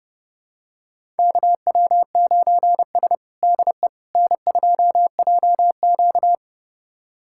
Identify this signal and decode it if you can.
Morse code, heard as a keyed tone, about 30 wpm, 700 Hz. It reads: KW9S DE N2JQ
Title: USA Callsign DE Callsign - 4 Characters - 30wpm